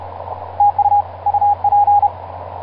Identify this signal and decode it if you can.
Morse code, heard as a keyed tone, about 31 wpm, 820 Hz, with radio noise. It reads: TU UP
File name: pj2-oh1vr iz2amw etc 12m cw 24mar2010 1955